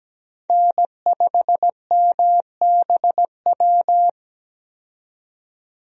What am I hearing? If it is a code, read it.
Morse code, heard as a keyed tone, about 17 wpm, 700 Hz. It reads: N5MBW
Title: USA Call Signs - 5 Characters - 17wpm